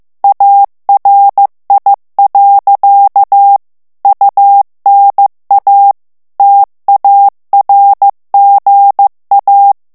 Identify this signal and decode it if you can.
Morse code, heard as a keyed tone, about 15 wpm, 800 Hz. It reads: ARI. UNA TARGA
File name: morse mp3 / ITALIEN texte clair 2 vitesse 14 mots minute